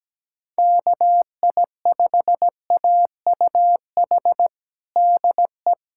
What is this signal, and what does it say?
Morse code, heard as a keyed tone, about 17 wpm, 700 Hz. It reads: KI5AUH DE
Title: USA Callsign DE Callsign 17wpm